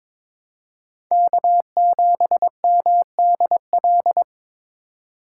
Morse code, heard as a keyed tone, about 22 wpm, 700 Hz. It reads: K7MDL
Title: USA Call Signs - 5 Characters - 3x Character Spacing - 22wpm